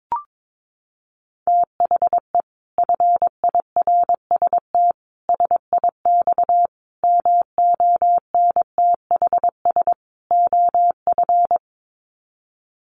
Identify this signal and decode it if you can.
Morse code, heard as a keyed tone, about 22 wpm, 700 Hz. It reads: THE FIRST SIX MONTHS OF
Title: Sets of 5 Words Encore2 22wpm